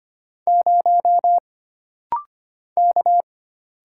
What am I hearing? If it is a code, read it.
Morse code, heard as a keyed tone, about 25 wpm, 700 Hz. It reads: 0 K
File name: Single Character (Letter:Number:Punctuation:Prosign) - Rapid-Fire - 25wpm